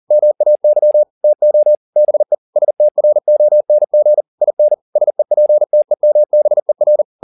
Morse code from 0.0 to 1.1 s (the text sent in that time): MAY